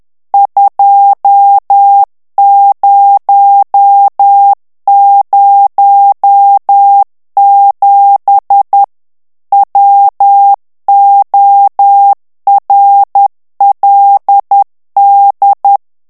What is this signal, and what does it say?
Morse code, heard as a keyed tone, about 10 wpm, 800 Hz. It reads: 2007 WORLD
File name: ANGLAIS texte clair 3 vitesse 10 mots minute